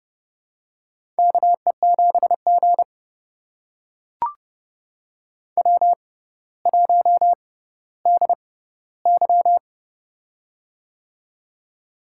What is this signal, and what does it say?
Morse code, heard as a keyed tone, about 29 wpm, spaced as for 15 wpm, 700 Hz. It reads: KE7G W1DY
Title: USA Call Signs - 4 Characters - 6x Character Spacing - 30wpm